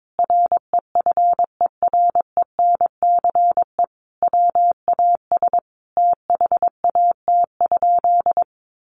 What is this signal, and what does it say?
Morse code, heard as a keyed tone, about 22 wpm, 700 Hz. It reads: REFERENCE WAS THAT?
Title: Sentences from Top 1200 Words 22wpm